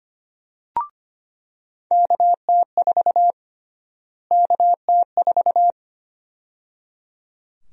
Morse code, heard as a keyed tone, about 25 wpm, 700 Hz. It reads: KT4 KT4
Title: USA Call Sign Prefixes - Repeated 2x - 25wpm